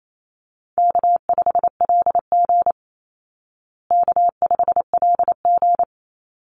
Morse code, heard as a keyed tone, about 28 wpm, 700 Hz. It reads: K5LG K5LG
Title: Easy Call Signs Repeated 3x - 28wpm